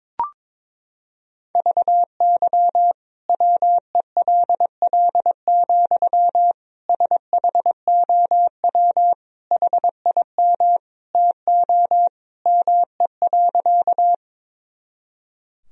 Morse code, heard as a keyed tone, about 21 wpm, 700 Hz. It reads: VY WELL, SHOW HIM TO ME.
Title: Sentences from Top 200 Words and Abbreviations 22wpm